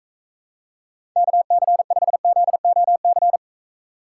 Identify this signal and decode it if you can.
Morse code, heard as a keyed tone, about 43 wpm, 700 Hz. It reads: KC5ZOC